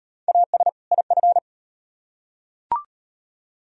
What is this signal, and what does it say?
Morse code, heard as a keyed tone, about 38 wpm, 700 Hz. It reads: AS IF